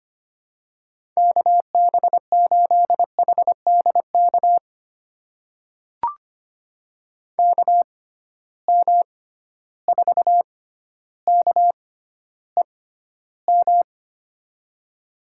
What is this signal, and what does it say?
Morse code, heard as a keyed tone, about 25 wpm, spaced as for 13 wpm, 700 Hz. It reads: KB8HDK KM4KEM
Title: USA Call Signs - 6 Characters - 6x Character Spacing - 25wpm